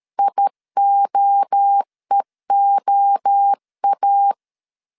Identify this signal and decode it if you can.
Morse code, heard as a keyed tone, about 13 wpm, 800 Hz. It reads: IOEOA
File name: CW / DET / 1 3 1500C S B 12wpm 2828